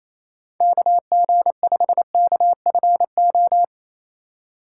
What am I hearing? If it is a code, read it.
Morse code, heard as a keyed tone, about 28 wpm, 700 Hz. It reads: KG5KFO
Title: USA Call Signs - 3x Character Spacing - 28wpm